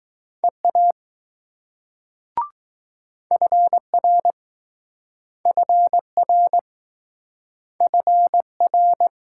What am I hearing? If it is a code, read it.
Morse code, heard as a keyed tone, about 20 wpm, 700 Hz. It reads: EA FR FR FR